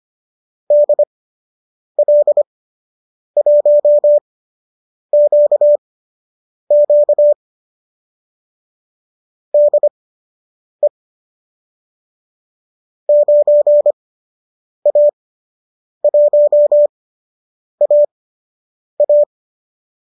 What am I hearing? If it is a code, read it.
Morse code, heard as a keyed tone, about 25 wpm, spaced as for 8 wpm, 590 Hz. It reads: DL1QQ DE 9A1AA